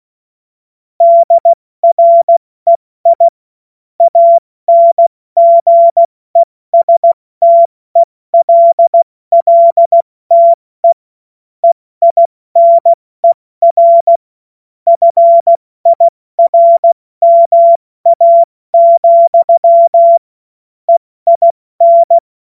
Morse code, heard as a keyed tone, about 16 wpm, 680 Hz. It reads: DREI ANGESTELLTE EINER FIRMA, EIN
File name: Morsekurs / Witze / Witz 14. Juni 2022 70 F80